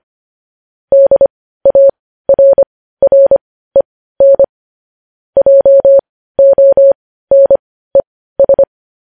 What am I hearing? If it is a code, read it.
Morse code, heard as a keyed tone, about 25 wpm, spaced as for 15 wpm, 550 Hz. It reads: DARREN JONES
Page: news-headlines-in-morse-code-at-15-wpm